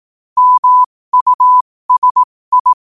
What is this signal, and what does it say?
Morse code, heard as a keyed tone, about 18 wpm, spaced as for 15 wpm, 1000 Hz. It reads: MUSI
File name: qotdfiles / 15wpm